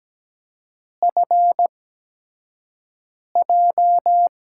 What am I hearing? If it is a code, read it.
Morse code, heard as a keyed tone, about 17 wpm, spaced as for 5 wpm, 700 Hz. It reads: FJ
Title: USA Call Signs - 8x Character Spacing - 17wpm